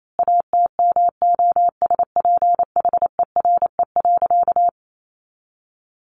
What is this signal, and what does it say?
Morse code, heard as a keyed tone, about 28 wpm, 700 Hz. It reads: ATMOSPHERE.
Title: A Sound Of Thunder 28wpm